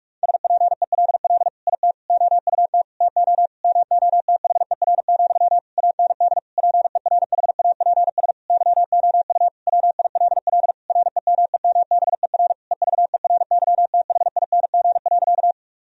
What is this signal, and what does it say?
Morse code, heard as a keyed tone, about 46 wpm, 700 Hz. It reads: SPELL IT OUT TO MOTHER, AND PERHAPS YOU WILL REMEMBER EVERYTHING.
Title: Sentences from Top 1000 Words - Review Part 1 - 45wpm